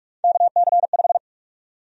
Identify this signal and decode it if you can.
Morse code, heard as a keyed tone, about 44 wpm, 700 Hz. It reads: KC5
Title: USA Call Sign Prefixes - Repeated 2x - 45wpm